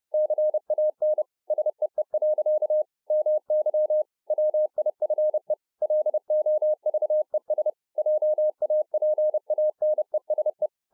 Morse code, heard as a keyed tone, about 30 wpm, 610 Hz. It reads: CAN SEE. MY WIFE LOVES JAPANESE